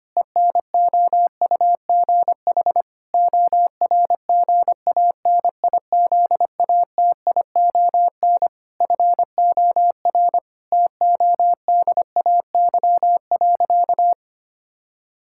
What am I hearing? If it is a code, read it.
Morse code, heard as a keyed tone, about 25 wpm, 700 Hz. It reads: ENOUGH ORGANIZATION FOR TODAY.
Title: Sentences from Top 1000 Words - Review Part 5 - 25wpm